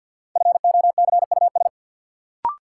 Morse code, heard as a keyed tone, about 51 wpm, 700 Hz. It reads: FOCUS
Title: Sentences from Top 1000 Words - Review Part 4 - New Word Drill - 50wpm